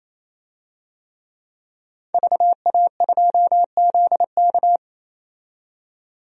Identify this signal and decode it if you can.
Morse code, heard as a keyed tone, about 28 wpm, 700 Hz. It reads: VA2ZK
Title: DXCC-100 Call Signs - 5 Characters - 28wpm